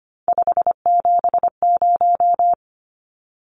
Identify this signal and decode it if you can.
Morse code, heard as a keyed tone, about 25 wpm, 690 Hz. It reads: 570
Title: Numbers - 3-Digit 25wpm